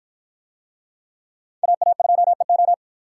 Morse code, heard as a keyed tone, about 53 wpm, 700 Hz. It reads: AA2EO